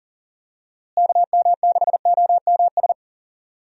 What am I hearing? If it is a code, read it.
Morse code, heard as a keyed tone, about 40 wpm, 700 Hz. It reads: KM6OMS
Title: USA Call Signs - 6 Characters - Repeated 2x - 40wpm